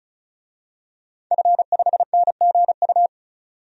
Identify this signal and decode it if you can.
Morse code, heard as a keyed tone, about 36 wpm, 700 Hz. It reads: F5NGU
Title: DXCC-100 Call Signs - 5 Characters - 35wpm